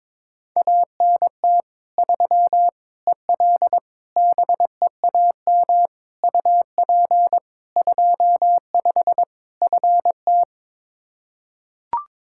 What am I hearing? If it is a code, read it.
Morse code, heard as a keyed tone, about 22 wpm, 700 Hz. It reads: ANT 3 EL BEAM UP 25 FT
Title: Practice QSO - Exchange 2 - Info - 22wpm